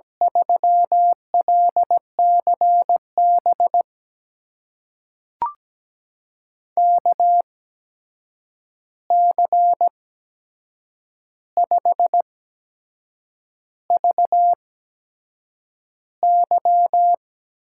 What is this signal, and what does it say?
Morse code, heard as a keyed tone, about 17 wpm, spaced as for 7 wpm, 700 Hz. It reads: 3LCB KC5VY
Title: USA Call Signs - 6 Characters - 8x Character Spacing - 17wpm